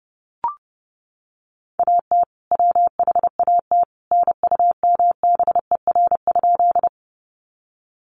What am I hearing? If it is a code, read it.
Morse code, heard as a keyed tone, about 30 wpm, 700 Hz. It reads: AT WHAT NUMBER?
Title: Sentences from Top 100 Words 30wpm